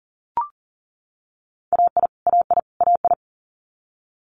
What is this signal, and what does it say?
Morse code, heard as a keyed tone, about 40 wpm, 700 Hz. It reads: AI AI AI
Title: Multi-letter Phonemes Repeated 3x 40wpm